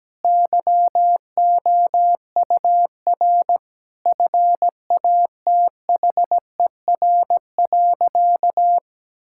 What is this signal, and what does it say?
Morse code, heard as a keyed tone, about 17 wpm, 700 Hz. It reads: YOUR FATHER.
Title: Sentences from Top 1000 Words - Review Part 6 - 17wpm